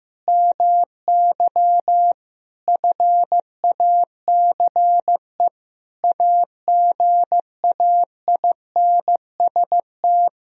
Morse code, heard as a keyed tone, about 15 wpm, 700 Hz. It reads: MY FACE AGAINST
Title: Sentences from Top 1000 Words - Review Part 2 - 15wpm